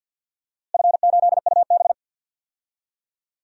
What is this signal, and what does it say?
Morse code, heard as a keyed tone, about 51 wpm, 700 Hz. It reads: F8UB
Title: DXCC-100 Call Signs - 4 Characters - Repeated 2x - 50wpm